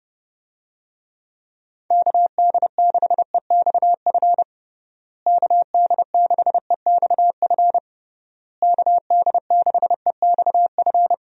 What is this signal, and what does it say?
Morse code, heard as a keyed tone, about 32 wpm, 700 Hz. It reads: KD6EXF KD6EXF KD6EXF
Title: USA Call Signs - 6 Characters - Repeated 3x - 30wpm